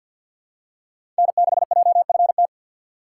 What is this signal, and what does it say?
Morse code, heard as a keyed tone, about 50 wpm, 700 Hz. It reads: N6JFT
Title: USA Call Signs - 4x Character Spacing - 50wpm